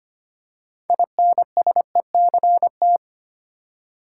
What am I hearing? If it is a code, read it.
Morse code, heard as a keyed tone, about 25 wpm, 700 Hz. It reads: INSECT
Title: Six Letter Words - 25wpm